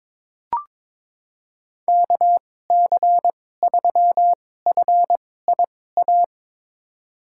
Morse code, heard as a keyed tone, about 22 wpm, spaced as for 16 wpm, 700 Hz. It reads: KC3FIA
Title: USA Call Signs - 6 Characters - 2x Character Spacing - 22wpm